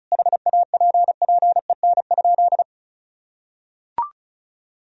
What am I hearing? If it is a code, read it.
Morse code, heard as a keyed tone, about 36 wpm, 700 Hz. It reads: HAPPEN?
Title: Sentences from Top 400 Words and Abbreviations 35wpm